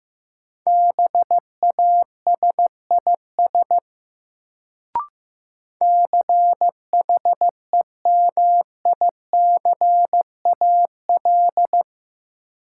Morse code, heard as a keyed tone, about 15 wpm, 700 Hz. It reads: BASIS CHEMICAL
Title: Sentences from Top 1200 Words - New Word Drill Alt - 15wpm